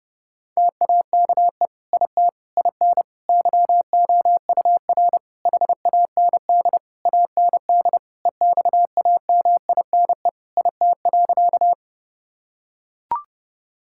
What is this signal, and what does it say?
Morse code, heard as a keyed tone, about 30 wpm, 700 Hz. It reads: TAKE IT IN YOUR HAND AND EXAMINE IT.
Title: Sentences from Top 1600 Words 30wpm